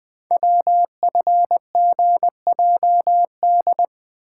Morse code, heard as a keyed tone, about 20 wpm, 700 Hz. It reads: WFGJD
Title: Code Groups - 5 Letter:Numbers - 20wpm